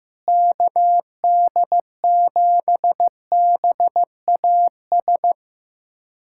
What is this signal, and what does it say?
Morse code, heard as a keyed tone, about 15 wpm, 700 Hz. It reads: KD7BAS
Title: USA Callsign DE Callsign - 6 Characters - 15wpm